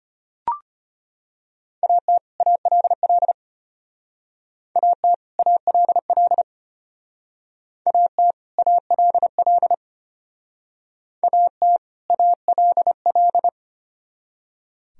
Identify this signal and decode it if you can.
Morse code, heard as a keyed tone, about 31 wpm, 700 Hz. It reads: AT ALL AT ALL AT ALL AT ALL